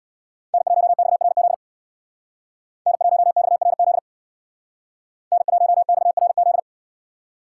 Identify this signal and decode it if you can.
Morse code, heard as a keyed tone, about 68 wpm, 700 Hz. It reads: N1XRZ N1XRZ N1XRZ